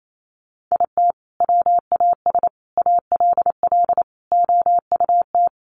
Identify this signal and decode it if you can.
Morse code, heard as a keyed tone, about 28 wpm, 700 Hz. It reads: IT WAS ALL OUT
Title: Sentences from Top 1200 Words 28wpm